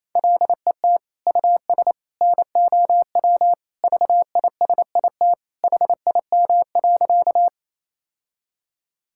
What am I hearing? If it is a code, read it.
Morse code, heard as a keyed tone, about 28 wpm, 700 Hz. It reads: LET US NOW VISIT HIM.
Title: Sentences from Top 1000 Words - Review Part 6 - 28wpm